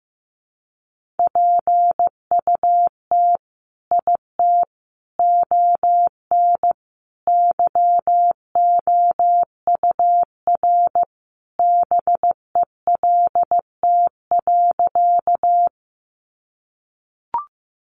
Morse code, heard as a keyed tone, about 15 wpm, 700 Hz. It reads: PUT IT ON YOUR BELT.
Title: Sentences from Top 1400 Words 15wpm